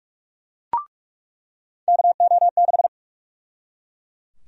Morse code, heard as a keyed tone, about 42 wpm, 700 Hz. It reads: KO6
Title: USA Call Sign Prefixes - 45wpm